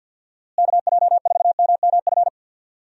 Morse code, heard as a keyed tone, about 51 wpm, 700 Hz. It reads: KJ4MMF
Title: USA Call Signs - 6 Characters - Repeated 3x - 50wpm